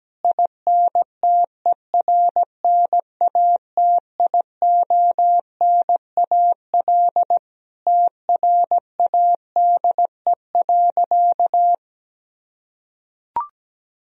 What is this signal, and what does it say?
Morse code, heard as a keyed tone, about 17 wpm, 700 Hz. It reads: INTERNATIONAL TRADE.
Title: Sentences from Top 1000 Words - Review Part 4 - 17wpm